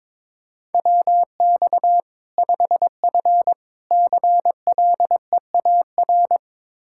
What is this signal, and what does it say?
Morse code, heard as a keyed tone, about 22 wpm, 700 Hz. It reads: WX 5F CLEAR
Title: Practice QSO - Exchange 2 - Info - 22wpm